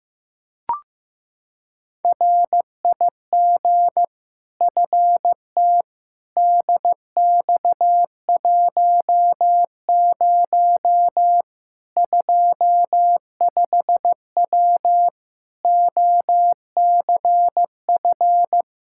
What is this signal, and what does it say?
Morse code, heard as a keyed tone, about 15 wpm, 700 Hz. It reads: RIG FT DX10 25W OCF
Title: Practice QSO - Exchange 2 - Info - 15wpm